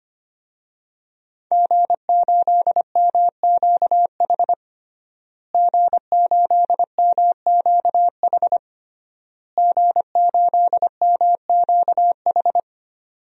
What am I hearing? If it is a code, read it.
Morse code, heard as a keyed tone, about 25 wpm, 700 Hz. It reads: G8MQH G8MQH G8MQH
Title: DXCC-100 Call Signs - 5 Characters - Repeated 3x - 25wpm